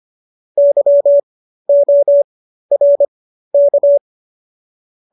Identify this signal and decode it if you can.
Morse code, heard as a keyed tone, about 25 wpm, spaced as for 13 wpm, 570 Hz. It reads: YORK